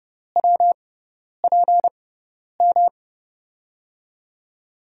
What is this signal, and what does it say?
Morse code, heard as a keyed tone, about 30 wpm, spaced as for 10 wpm, 700 Hz. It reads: WPM
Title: USA Call Signs - 5 Characters - 6x Character Spacing - 30wpm